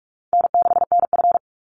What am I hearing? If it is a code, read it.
Morse code, heard as a keyed tone, about 46 wpm, 700 Hz. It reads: N6NF
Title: Easy Call Signs - 45wpm